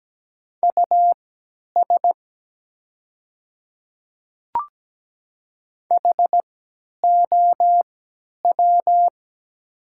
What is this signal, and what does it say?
Morse code, heard as a keyed tone, about 17 wpm, spaced as for 10 wpm, 700 Hz. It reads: US HOW
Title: 3x Character Spacing - Top 200 Words - 17wpm